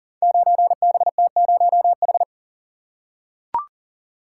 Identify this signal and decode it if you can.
Morse code, heard as a keyed tone, about 37 wpm, 700 Hz. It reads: 9BT0H
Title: Code Groups - 5 Letter:Numbers - 40wpm